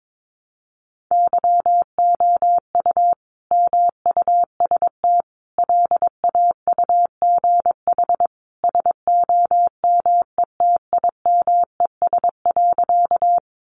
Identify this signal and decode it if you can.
Morse code, heard as a keyed tone, about 22 wpm, 700 Hz. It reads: YOU MUST LAUGH SOMETIMES.
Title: Sentences from Top 600 Words 22wpm